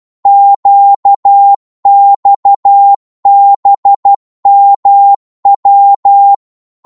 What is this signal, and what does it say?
Morse code, heard as a keyed tone, about 12 wpm, 800 Hz. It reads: QXBMW
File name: Morse / ltr-12wpm